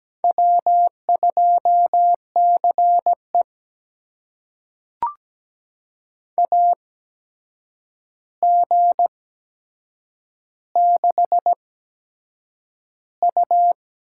Morse code, heard as a keyed tone, about 17 wpm, spaced as for 7 wpm, 700 Hz. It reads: W2CE AG6U
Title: USA Call Signs - 4 Characters - 8x Character Spacing - 17wpm